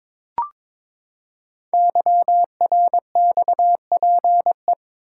Morse code, heard as a keyed tone, about 22 wpm, 700 Hz. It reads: YRXPE
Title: Code Groups - 5 Letters - 22wpm